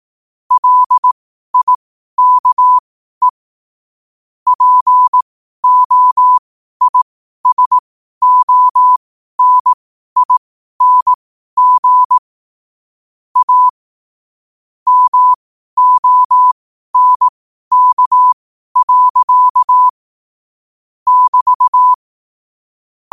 Morse code, heard as a keyed tone, about 18 wpm, spaced as for 12 wpm, 1000 Hz. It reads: LIKE POISONING A MONK. <BT>
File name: qotdfiles / 12wpm